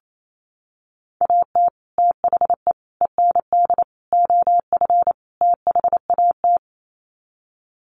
Morse code, heard as a keyed tone, about 28 wpm, 700 Hz. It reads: AT THE END OF THAT
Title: Sets of 5 Words Encore2 28wpm